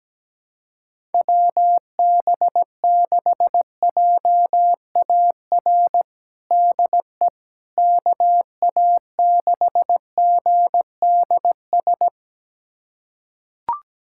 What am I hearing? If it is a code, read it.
Morse code, heard as a keyed tone, about 17 wpm, 700 Hz. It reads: WB6JAR DE KA6GDS
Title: USA Callsign DE Callsign - 6 Characters - 17wpm